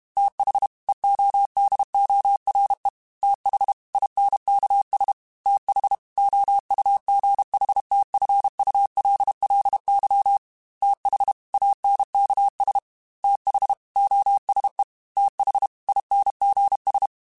Morse code, heard as a keyed tone, about 32 wpm, 800 Hz. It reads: TH EODORE TH INKS TH OUGHTFULLY TH ANKS TH OSE TH INGS
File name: TH Runs sp @30